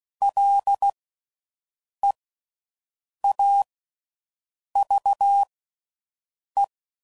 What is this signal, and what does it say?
Morse code, heard as a keyed tone, about 16 wpm, spaced as for 6 wpm, 800 Hz. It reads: LEAVE